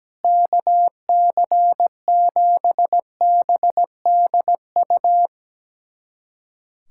Morse code, heard as a keyed tone, about 17 wpm, 700 Hz. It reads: KC7BDU
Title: USA Callsign DE Callsign - 6 Characters - 17wpm